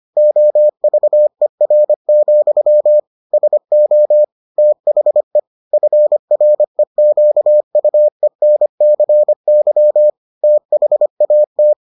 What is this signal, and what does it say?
Morse code, heard as a keyed tone, about 25 wpm, 600 Hz. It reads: OVER, SO THE FREQUENCY THAT